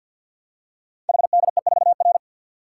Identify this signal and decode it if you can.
Morse code, heard as a keyed tone, about 48 wpm, 700 Hz. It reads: HDE4R
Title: Code Groups - 5 Letter:Numbers - 50wpm